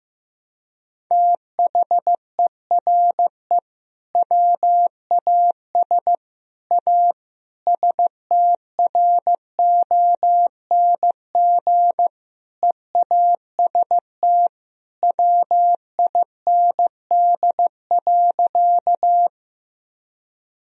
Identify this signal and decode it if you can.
Morse code, heard as a keyed tone, about 15 wpm, 700 Hz. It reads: THERE WAS A STRONG EAST WIND.
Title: Sentences from Top 1000 Words - Review Part 3 - 15wpm